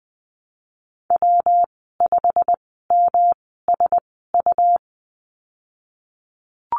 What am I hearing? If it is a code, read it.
Morse code, heard as a keyed tone, about 20 wpm, spaced as for 15 wpm, 700 Hz. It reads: W5MSU
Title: 122 Lesson 20g - Number 3 - Call Signs 20wpm